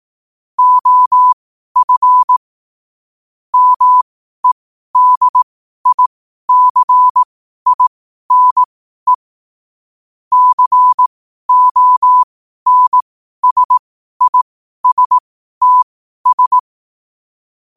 Morse code, heard as a keyed tone, about 18 wpm, spaced as for 12 wpm, 1000 Hz. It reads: OF MEDICINE CONSISTS
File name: qotdfiles / 12wpm